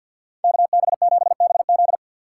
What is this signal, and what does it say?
Morse code, heard as a keyed tone, about 62 wpm, 700 Hz. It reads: KB7BZ
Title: Speed Racing - USA 2x2 Call Signs 50wpm